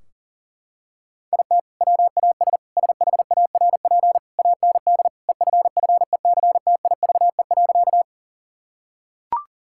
Morse code, heard as a keyed tone, about 40 wpm, 700 Hz. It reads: IT WAS SHARP AND EFFECTIVE.
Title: Sentences from Top 1600 Words 40wpm